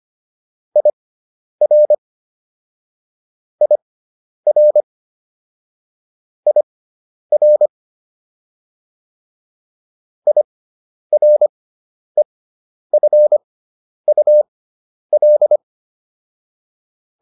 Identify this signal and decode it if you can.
Morse code, heard as a keyed tone, about 25 wpm, spaced as for 10 wpm, 600 Hz. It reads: IR IR IR IREFUL